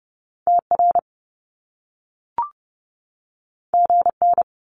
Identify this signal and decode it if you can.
Morse code, heard as a keyed tone, about 30 wpm, 700 Hz. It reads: TR GN
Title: Top Two Letter Combinations in Words 30wpm